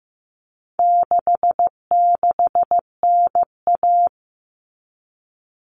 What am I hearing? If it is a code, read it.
Morse code, heard as a keyed tone, about 15 wpm, 700 Hz. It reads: 66NA
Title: All Cut Numbers - 4-Digit 15wpm